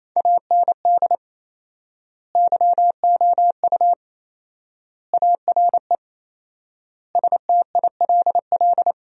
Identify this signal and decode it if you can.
Morse code, heard as a keyed tone, about 28 wpm, 700 Hz. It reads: AND YOU ARE STILL
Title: 4x Word-Spacing - Sentences from Top 400 Words - 28wpm